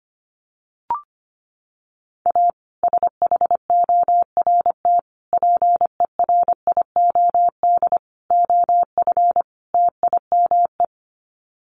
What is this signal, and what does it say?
Morse code, heard as a keyed tone, about 25 wpm, 700 Hz. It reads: A SHORT PERIOD OF TIME
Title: Sets of 5 Words Encore 25wpm